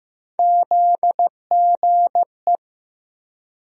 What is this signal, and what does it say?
Morse code, heard as a keyed tone, about 15 wpm, 700 Hz. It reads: ZGE
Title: USA Call Sign Postfixes - 3 Letters 15wpm